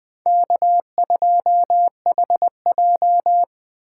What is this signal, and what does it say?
Morse code, heard as a keyed tone, about 20 wpm, 700 Hz. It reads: K2HJ
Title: Speed Racing - USA Call Signs 17wpm